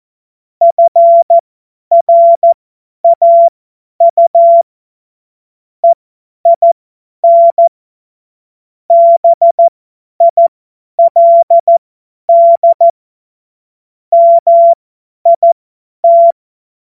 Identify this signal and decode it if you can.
Morse code, heard as a keyed tone, about 14 wpm, spaced as for 10 wpm, 680 Hz. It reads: FRAU EIN BILD MIT
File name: Witz-07-Oktober-2025-50-F70